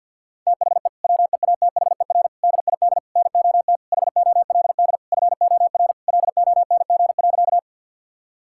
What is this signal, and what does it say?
Morse code, heard as a keyed tone, about 50 wpm, 700 Hz. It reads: THE WEATHER DID NOT HOLD FOR LONG.
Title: Sentences from Top 1000 Words - Review Part 5 - 50wpm